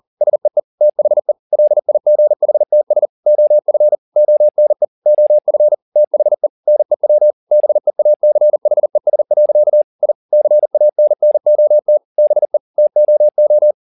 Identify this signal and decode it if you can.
Morse code, heard as a keyed tone, about 40 wpm, 600 Hz. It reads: SEE THE LIGHTS OF ONE OF THE NEW BEACHES. I CANNOT BE TOO
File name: the old man and the sea 2 40wpm